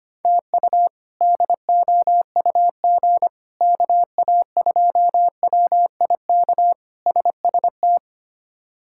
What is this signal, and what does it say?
Morse code, heard as a keyed tone, about 25 wpm, 700 Hz. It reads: TU DOUG KA2WIK SST
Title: K1USN SST QSO Practice - 25wpm